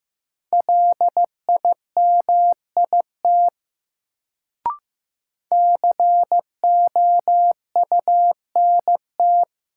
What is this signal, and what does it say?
Morse code, heard as a keyed tone, about 15 wpm, 700 Hz. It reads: LIMIT COUNT
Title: Sentences from Top 1500 Words - New Word Drill Alt - 15wpm